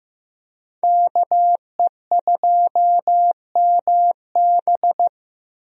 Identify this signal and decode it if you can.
Morse code, heard as a keyed tone, about 15 wpm, 700 Hz. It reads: KE2MB
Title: USA Call Signs - 5 Characters - Repeated 2x - 15wpm